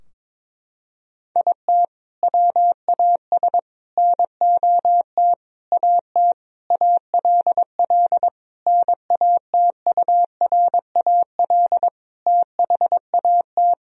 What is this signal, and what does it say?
Morse code, heard as a keyed tone, about 22 wpm, 700 Hz. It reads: IT WAS NOT AT ALL NATURAL THAT
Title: Sentences from Top 1000 Words - Review Part 4 - 22wpm